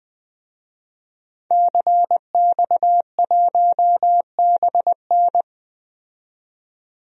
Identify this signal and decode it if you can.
Morse code, heard as a keyed tone, about 20 wpm, 700 Hz. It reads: CX1BN
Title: DXCC-100 Call Signs - 5 Characters - 20wpm